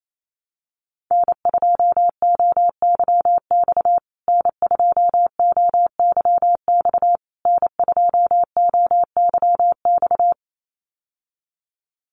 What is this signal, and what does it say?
Morse code, heard as a keyed tone, about 28 wpm, 700 Hz. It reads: N2OYX N2OYX N2OYX
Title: USA Call Signs Repeated 3x 28wpm